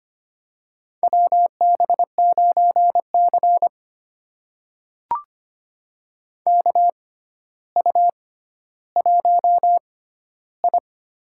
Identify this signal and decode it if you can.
Morse code, heard as a keyed tone, about 25 wpm, spaced as for 13 wpm, 700 Hz. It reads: WB9C KU1I
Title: USA Call Signs - 6x Character Spacing - 25wpm